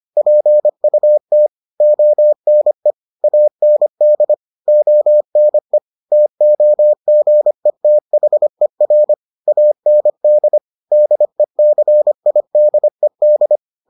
Morse code, heard as a keyed tone, about 25 wpm, 600 Hz. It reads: PUT ONE AND ONE TOGETHER AND DECIDED